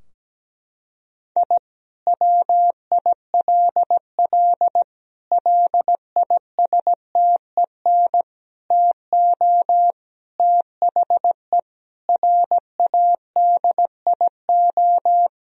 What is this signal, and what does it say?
Morse code, heard as a keyed tone, about 17 wpm, 700 Hz. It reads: I WILL LISTEN TO THE RADIO
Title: Sentences from Top 1000 Words - Review Part 3 - 17wpm